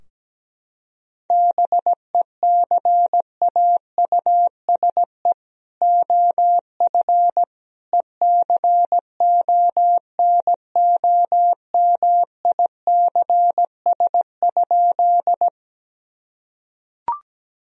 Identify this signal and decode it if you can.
Morse code, heard as a keyed tone, about 17 wpm, 700 Hz. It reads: BECAUSE OF ECONOMICS?
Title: Sentences from Top 1000 Words - Review Part 3 - 17wpm